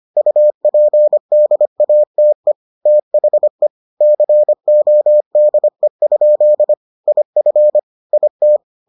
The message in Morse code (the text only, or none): UPDATE THE CODE? IF IT